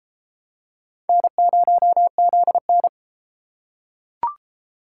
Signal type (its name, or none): Morse code